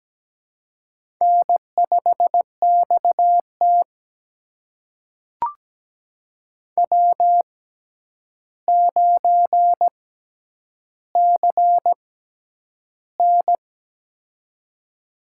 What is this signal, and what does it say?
Morse code, heard as a keyed tone, about 17 wpm, spaced as for 9 wpm, 700 Hz. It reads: N5XT W9CN
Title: USA Call Signs - 4 Characters - 6x Character Spacing - 17wpm